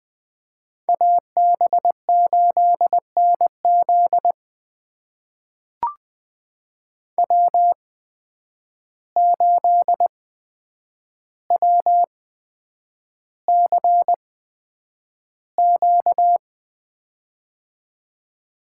Morse code, heard as a keyed tone, about 20 wpm, spaced as for 9 wpm, 700 Hz. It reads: AB8NZ W8WCQ
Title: USA Call Signs - 5 Characters - 8x Character Spacing - 20wpm